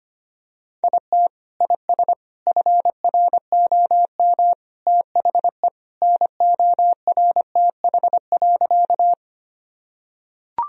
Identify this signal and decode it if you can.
Morse code, heard as a keyed tone, about 25 wpm, 700 Hz. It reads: IT IS FROM THE NORTH.
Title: Sentences from Top 1000 Words - Review Part 1 - 25wpm